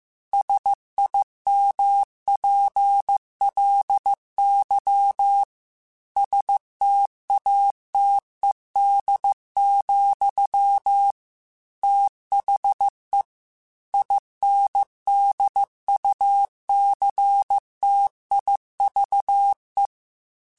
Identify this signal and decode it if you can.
Morse code, heard as a keyed tone, about 15 wpm, 800 Hz. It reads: SIMPLY STATED, THE INDUCTIVE
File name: Technical Subjects / Technical Subjects-mp3s / XL @13